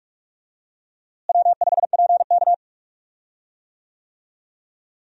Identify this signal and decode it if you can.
Morse code, heard as a keyed tone, about 43 wpm, 700 Hz. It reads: W5PK
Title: USA Call Signs - 4 Characters - Repeated 3x - 45wpm